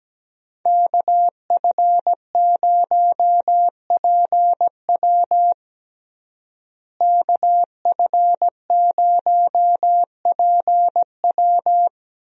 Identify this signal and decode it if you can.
Morse code, heard as a keyed tone, about 17 wpm, 700 Hz. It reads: KF0PW KF0PW
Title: USA Call Signs - 5 Characters - Repeated 3x - 17wpm